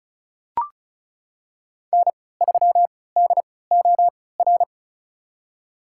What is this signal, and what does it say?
Morse code, heard as a keyed tone, about 34 wpm, spaced as for 20 wpm, 700 Hz. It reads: N3DOR
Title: USA Call Signs - 5 Characters - 3x Character Spacing - 35wpm